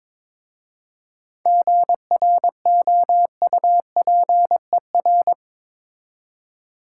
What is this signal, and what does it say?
Morse code, heard as a keyed tone, about 22 wpm, 700 Hz. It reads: GROUPER
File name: Difficult Words with Hints - 22wpm